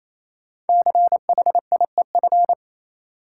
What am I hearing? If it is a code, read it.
Morse code, heard as a keyed tone, about 28 wpm, 700 Hz. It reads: CHIEF
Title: Sets of 5 Words Encore - New Word Drill - 28wpm